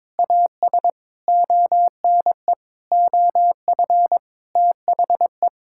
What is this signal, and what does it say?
Morse code, heard as a keyed tone, about 22 wpm, 700 Hz. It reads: AS ONE OF THE
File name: practice / 22wpm / Sets of 7 Words - 22wpm